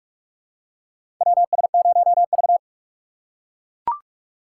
Morse code, heard as a keyed tone, about 42 wpm, 700 Hz. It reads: WS0V
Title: USA Call Signs - 3x Character Spacing - 45wpm